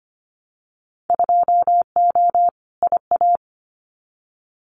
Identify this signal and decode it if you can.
Morse code, heard as a keyed tone, about 25 wpm, 690 Hz. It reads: 2O IA
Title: Winter Field Day Exchanges 25wpm